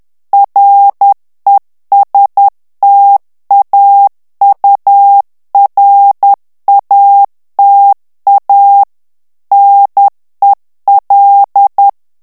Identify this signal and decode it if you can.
Morse code, heard as a keyed tone, about 10 wpm, 800 Hz. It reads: RESTAURATA NEL
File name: ITALIEN texte clair 1 vitesse 10 mots minute